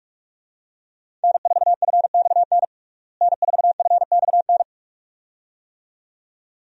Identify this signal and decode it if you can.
Morse code, heard as a keyed tone, about 42 wpm, 700 Hz. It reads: N4FXN N4FXN
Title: USA Call Signs - 5 Characters - Repeated 2x - 45wpm